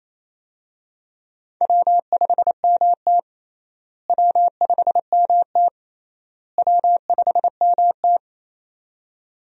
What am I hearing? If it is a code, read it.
Morse code, heard as a keyed tone, about 28 wpm, 700 Hz. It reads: W5MT W5MT W5MT
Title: USA Call Signs - 4 Characters - Repeated 3x - 28wpm